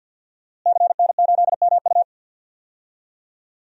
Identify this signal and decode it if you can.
Morse code, heard as a keyed tone, about 52 wpm, 700 Hz. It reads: CN8MU
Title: DXCC-100 Call Signs - 5 Characters - Repeated 3x - 50wpm